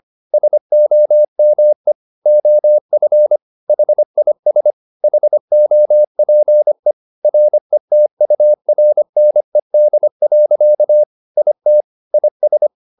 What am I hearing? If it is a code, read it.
Morse code, heard as a keyed tone, about 25 wpm, 600 Hz. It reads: SOME OF HIS HOPE RETURNED. IT IS